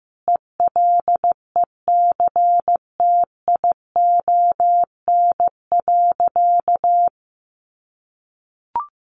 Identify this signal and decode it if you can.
Morse code, heard as a keyed tone, about 15 wpm, 700 Hz. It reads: ELECTION.
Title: Sentences from Top 1500 Words 15wpm